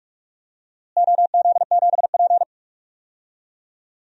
Morse code, heard as a keyed tone, about 41 wpm, 700 Hz. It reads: OZ7P
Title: DXCC-100 Call Signs - 4 Characters - Repeated 2x - 45wpm